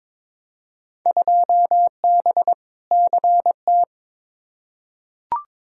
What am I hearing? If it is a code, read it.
Morse code, heard as a keyed tone, about 21 wpm, 700 Hz. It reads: 2B CT
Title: Field Day Exchanges Encore - 22wpm